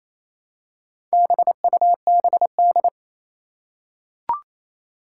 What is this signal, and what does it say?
Morse code, heard as a keyed tone, about 27 wpm, 700 Hz. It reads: BUBD
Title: Four Challenging Letter-Number Combinations 28wpm